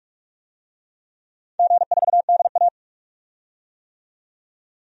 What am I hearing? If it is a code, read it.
Morse code, heard as a keyed tone, about 41 wpm, 700 Hz. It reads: G4DA
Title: DXCC-100 Call Signs - 4 Characters - 45wpm